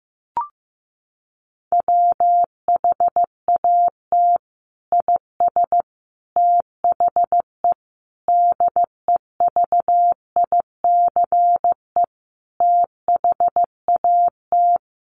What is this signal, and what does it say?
Morse code, heard as a keyed tone, about 15 wpm, 700 Hz. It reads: WHAT IS THE DEVICE THAT
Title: Sentences from Top 1500 Words 15wpm